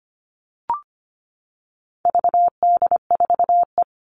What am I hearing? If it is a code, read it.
Morse code, heard as a keyed tone, about 25 wpm, 700 Hz. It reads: VD4E
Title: All Cut Numbers - 4-Digit 25wpm